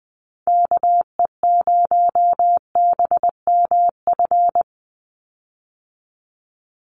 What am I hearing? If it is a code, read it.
Morse code, heard as a keyed tone, about 20 wpm, 700 Hz. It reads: KE0BMF
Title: USA Call Signs 20wpm